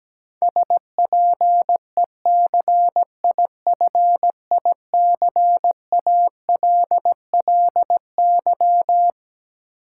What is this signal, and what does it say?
Morse code, heard as a keyed tone, about 17 wpm, 700 Hz. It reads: SPECIFICALLY
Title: Sentences from Top 1100 Words - New Word Drill - 17wpm